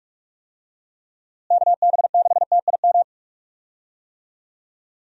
Morse code, heard as a keyed tone, about 42 wpm, 700 Hz. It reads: KB6TIM
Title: USA Call Signs - 6 Characters - Repeated 3x - 45wpm